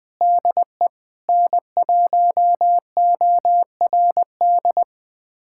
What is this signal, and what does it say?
Morse code, heard as a keyed tone, about 20 wpm, 700 Hz. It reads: DE N1ORD
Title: USA Callsign DE Callsign - 5 Characters - 20wpm